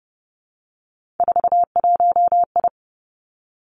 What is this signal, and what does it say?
Morse code, heard as a keyed tone, about 30 wpm, 700 Hz. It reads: 41I
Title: Three Challenging Letter-Number Combinations 30wpm